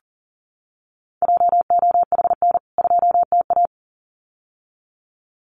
Morse code, heard as a keyed tone, about 40 wpm, 700 Hz. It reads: JOHN 2TA
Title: CWT Exchange 40wpm